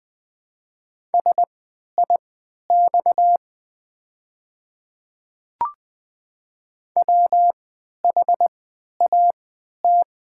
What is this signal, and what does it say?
Morse code, heard as a keyed tone, about 20 wpm, spaced as for 11 wpm, 700 Hz. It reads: SIX WHAT
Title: 3x Character Spacing - Top 500 Words - 20wpm